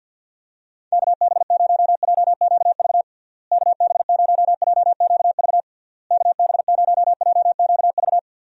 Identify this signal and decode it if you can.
Morse code, heard as a keyed tone, about 53 wpm, 700 Hz. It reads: KB0JQV KB0JQV KB0JQV
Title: USA Call Signs - 6 Characters - Repeated 3x - 50wpm